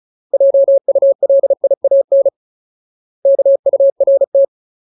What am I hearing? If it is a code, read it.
Morse code, heard as a keyed tone, about 34 wpm, 550 Hz. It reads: JULIAN KURT
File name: Ultimate Word List / 35WPM Ultimate